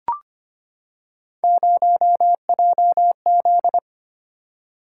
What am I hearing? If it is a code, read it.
Morse code, heard as a keyed tone, about 25 wpm, 700 Hz. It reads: 0JZ
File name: Three Letter-Number Combinations - 25wpm